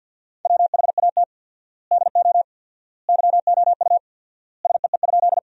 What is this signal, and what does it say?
Morse code, heard as a keyed tone, about 50 wpm, 700 Hz. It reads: WHAT DO YOU SEE?
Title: 4x Word-Spacing - Sentences from Top 100 Words - 50wpm